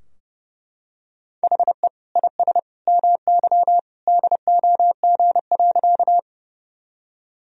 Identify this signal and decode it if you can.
Morse code, heard as a keyed tone, about 32 wpm, 700 Hz. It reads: HE IS MY DOG.
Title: Sentences from Top 1000 Words - Review Part 2 - 30wpm